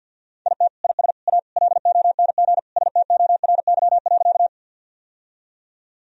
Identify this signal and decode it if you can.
Morse code, heard as a keyed tone, about 52 wpm, 700 Hz. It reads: IT IS A LONG STORY.
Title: Sentences from Top 1000 Words - Review Part 1 - 50wpm